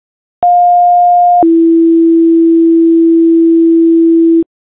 Two-tone page, heard as a one-second tone, then a three-second tone, about 710 then 330 Hz. It is heard from 0.4 to 4.4 s.